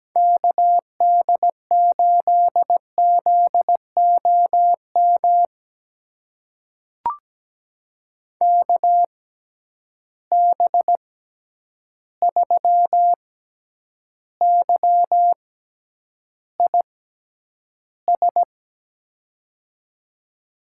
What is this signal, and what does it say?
Morse code, heard as a keyed tone, about 17 wpm, spaced as for 9 wpm, 700 Hz. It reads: KD8ZOM KB3YIS
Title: USA Call Signs - 6 Characters - 6x Character Spacing - 17wpm